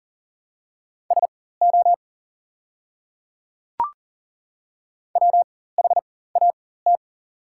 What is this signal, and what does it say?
Morse code, heard as a keyed tone, about 38 wpm, spaced as for 18 wpm, 700 Hz. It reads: SO WHAT
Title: 4x Character Spacing - Top 300 Words - 40wpm